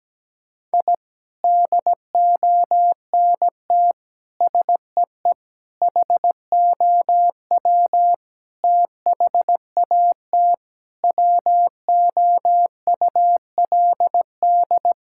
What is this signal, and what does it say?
Morse code, heard as a keyed tone, about 17 wpm, 700 Hz. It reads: I DONT SEE HOW THAT WOULD
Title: Sentences from Top 1000 Words - Review Part 6 - 17wpm